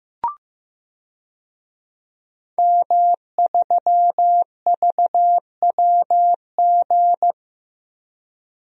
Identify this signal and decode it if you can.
Morse code, heard as a keyed tone, about 15 wpm, 700 Hz. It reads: M3VWG
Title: DXCC-100 Call Signs - 5 Characters - 15wpm